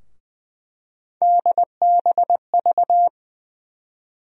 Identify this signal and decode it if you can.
Morse code, heard as a keyed tone, about 20 wpm, 700 Hz. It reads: DBV
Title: USA Call Sign Postfixes - 3 Letters 20wpm